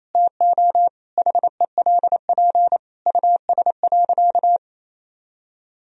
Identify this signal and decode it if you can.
Morse code, heard as a keyed tone, about 28 wpm, 700 Hz. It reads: TO HELP US.
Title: Sentences from Top 1000 Words - Review Part 2 - 28wpm